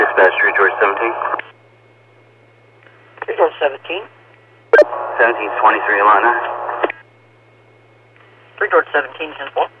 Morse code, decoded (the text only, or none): none